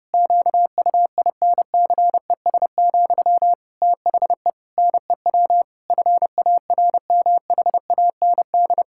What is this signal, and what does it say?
Morse code, heard as a keyed tone, about 30 wpm, 700 Hz. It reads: QUINCES, THE NEW FARMHAND
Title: Pangram Challenge v2 - 30wpm